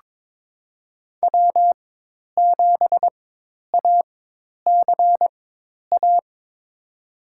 Morse code, heard as a keyed tone, about 22 wpm, spaced as for 10 wpm, 700 Hz. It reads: W7ACA
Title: USA Call Signs - 5 Characters - 4x Character Spacing - 22wpm